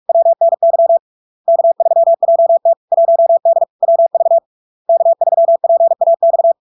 Morse code, heard as a keyed tone, about 46 wpm, 670 Hz. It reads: WNY K3JT 1D WV K3PAX
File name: LARK-FD-2022-Log-in-CW-45wpm-670-Hz